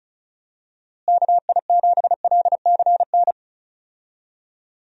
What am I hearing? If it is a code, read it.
Morse code, heard as a keyed tone, about 36 wpm, 700 Hz. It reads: KI7LCN
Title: USA Call Signs - 6 Characters - Repeated 2x - 35wpm